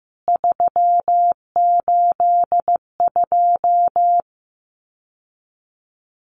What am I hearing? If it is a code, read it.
Morse code, heard as a keyed tone, about 15 wpm, 700 Hz. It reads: 382
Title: CWT Exchange 15wpm